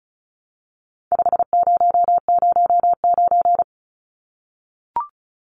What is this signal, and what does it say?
Morse code, heard as a keyed tone, about 34 wpm, 700 Hz. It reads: 5009
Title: Numbers - 4-Digit 35wpm